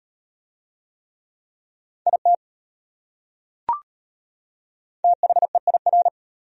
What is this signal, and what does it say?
Morse code, heard as a keyed tone, about 32 wpm, 700 Hz. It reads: IT THEIR